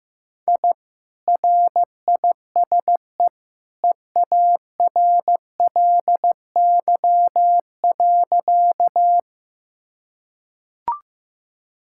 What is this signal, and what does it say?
Morse code, heard as a keyed tone, about 15 wpm, 700 Hz. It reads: I RISE EARLY.
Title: Sentences from Top 1000 Words - Review Part 6 - 15wpm